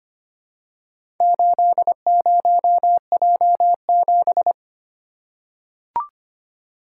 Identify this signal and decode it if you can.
Morse code, heard as a keyed tone, about 25 wpm, 700 Hz. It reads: 80J7
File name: practice / 25wpm / Four Letter-Number Combinations - 25wpm